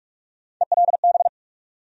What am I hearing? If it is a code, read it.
Morse code, heard as a keyed tone, about 43 wpm, 700 Hz. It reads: ELB
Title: USA Call Sign Postfixes - 3 Letters 45wpm